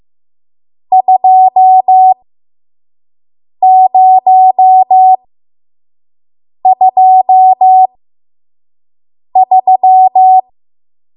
Morse code, heard as a keyed tone, about 15 wpm, spaced as for 5 wpm, 750 Hz. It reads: 2023
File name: Morse / 05WPM